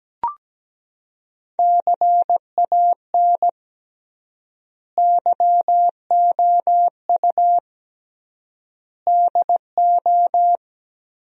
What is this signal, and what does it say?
Morse code, heard as a keyed tone, about 17 wpm, 700 Hz. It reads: CAN YOU DO
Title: 3x Word-Spacing - Sentences from Top 100 Words - 17wpm